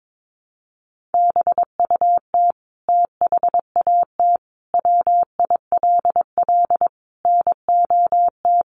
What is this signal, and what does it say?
Morse code, heard as a keyed tone, about 22 wpm, 700 Hz. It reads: BUT THAT WILL NOT
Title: Sentences from Top 900 Words 22wpm